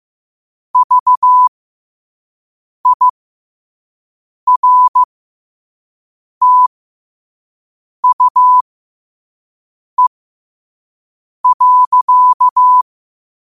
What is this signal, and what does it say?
Morse code, heard as a keyed tone, about 15 wpm, spaced as for 5 wpm, 1000 Hz. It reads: VIRTUE.
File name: qotdfiles / 5wpm